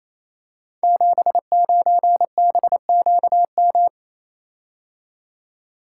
Morse code, heard as a keyed tone, about 28 wpm, 700 Hz. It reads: 79BQM
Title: Code Groups - 5 Letter:Numbers - 28wpm